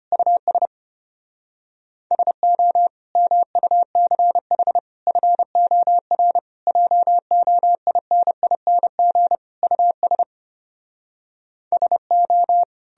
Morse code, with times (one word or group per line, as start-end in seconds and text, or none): 0.1-0.7 US
2.1-2.9 SO
3.2-4.8 MUCH
5.1-6.4 FOR
6.7-9.4 JOINING
9.6-10.2 US
11.7-12.6 SO